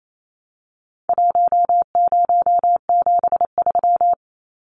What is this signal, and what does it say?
Morse code, heard as a keyed tone, about 28 wpm, 700 Hz. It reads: 1073
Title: Numbers - 4-Digit 28wpm